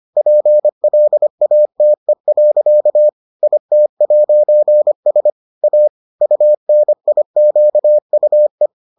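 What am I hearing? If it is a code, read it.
Morse code, heard as a keyed tone, about 25 wpm, 600 Hz. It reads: PLATE. IT'S A UNIQUE
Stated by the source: Morse edition of the Foundations of Amateur Radio episodes.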